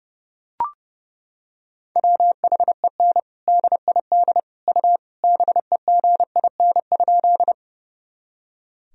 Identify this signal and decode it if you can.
Morse code, heard as a keyed tone, about 32 wpm, 700 Hz. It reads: WHEN DID U BEGIN?
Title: Sentences from Top 300 Words and Abbreviations 30wpm